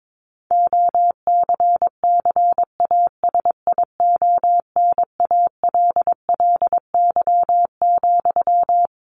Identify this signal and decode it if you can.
Morse code, heard as a keyed tone, about 22 wpm, 700 Hz. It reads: OCCASIONALLY,
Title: Sentences from Top 1400 Words 22wpm